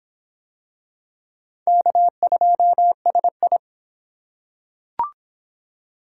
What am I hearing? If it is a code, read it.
Morse code, heard as a keyed tone, about 26 wpm, 700 Hz. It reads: K2SI